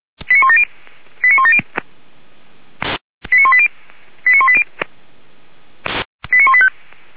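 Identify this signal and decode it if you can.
Five-tone selective call (ZVEI), heard as quick runs of five tones, tones of about 70 ms. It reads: 89170 89170 89180 89180 89185
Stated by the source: Five-tone ZVEI selective call 89136, received on SDR, recorded by feuerwehr-orlamuende (fire brigade Orlamünde).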